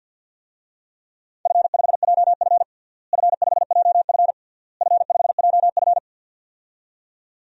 Morse code, heard as a keyed tone, about 47 wpm, 700 Hz. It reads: F5JF F5JF F5JF
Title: DXCC-100 Call Signs - 4 Characters - Repeated 3x - 50wpm